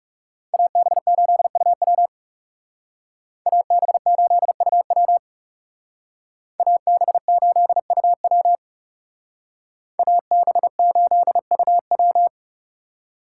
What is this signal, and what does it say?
Morse code, heard as a keyed tone, about 36 wpm, 700 Hz. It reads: AB8UW AB8UW AB8UW AB8UW
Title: Speed Racing - USA 2x2 Call Signs 30wpm